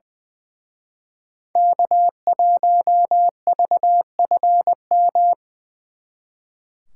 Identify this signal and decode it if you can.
Morse code, heard as a keyed tone, about 25 wpm, 700 Hz. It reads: K1VFM
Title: Speed Racing - USA 1x3 Call Signs 20wpm